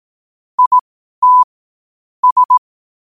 Morse code, heard as a keyed tone, about 18 wpm, spaced as for 12 wpm, 1000 Hz. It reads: IT S
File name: qotdfiles / 12wpm